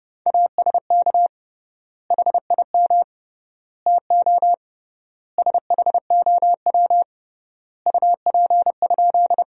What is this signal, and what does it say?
Morse code, heard as a keyed tone, about 30 wpm, 700 Hz. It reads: ASK HIM TO SHOW UP?
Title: 3x Word-Spacing - Sentences from Top 200 Words - 30wpm